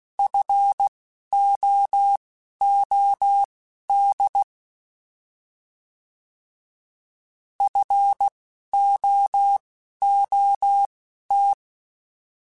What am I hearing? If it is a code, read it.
Morse code, heard as a keyed tone, about 16 wpm, spaced as for 11 wpm, 800 Hz. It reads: FOOD FOOT
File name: Alphabetical / Alphabetical mp3s / Fs @10